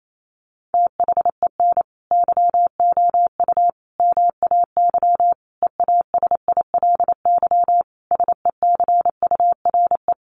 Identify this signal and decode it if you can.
Morse code, heard as a keyed tone, about 28 wpm, 700 Hz. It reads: THEN YOU MAY EASILY SECURE
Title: Sentences from Top 1200 Words 28wpm